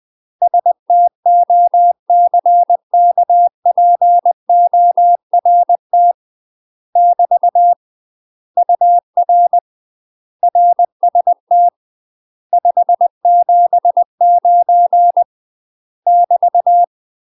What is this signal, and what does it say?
Morse code, heard as a keyed tone, about 20 wpm, 700 Hz. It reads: STOCKPORT <BT> UR RST 579 <BT>
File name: UK MASTER QSOs TRAINING / 20WPM / UK MASTER QSOs 0014